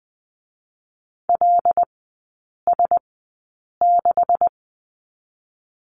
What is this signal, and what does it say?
Morse code, heard as a keyed tone, about 19 wpm, 700 Hz. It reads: L S 6
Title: 205 Lesson 33e - Letter j - Three Characters 20wpm